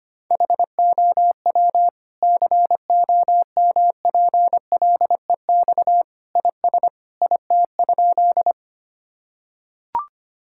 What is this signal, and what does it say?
Morse code, heard as a keyed tone, about 25 wpm, 700 Hz. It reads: HOW COMPLEX IS IT?
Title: Sentences from Top 1000 Words - Review Part 6 - 25wpm